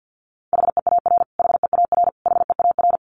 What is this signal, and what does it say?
Morse code, heard as a keyed tone, about 50 wpm, 700 Hz. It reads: HEAR HEAR HEAR
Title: Top 500 Words Repeated 50wpm